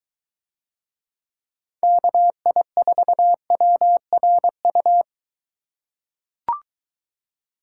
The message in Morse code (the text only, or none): KI4WRU